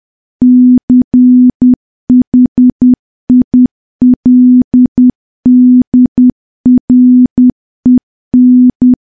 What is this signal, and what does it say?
Morse code, heard as a keyed tone, about 10 wpm, 250 Hz. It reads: CHILDREN
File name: morse / morse-training-10WPM